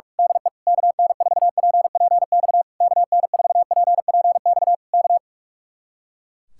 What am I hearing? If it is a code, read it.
Morse code, heard as a keyed tone, about 46 wpm, 700 Hz. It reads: DE KN4PPX KN4PPX K
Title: Speed-Racing Common QSO Phrases Encore - 45wpm